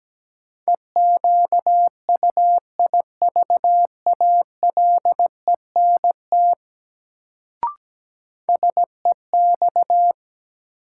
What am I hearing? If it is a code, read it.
Morse code, heard as a keyed tone, about 17 wpm, 700 Hz. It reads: EQUIVALENT SEX
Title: Sentences from Top 1600 Words - New Word Drill Alt - 17wpm